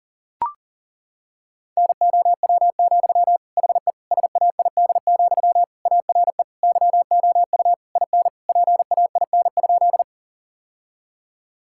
Morse code, heard as a keyed tone, about 40 wpm, 700 Hz. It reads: NOW, HE SAID, ARE YOU IN PAIN?
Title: Sentences from Top 1600 Words 40wpm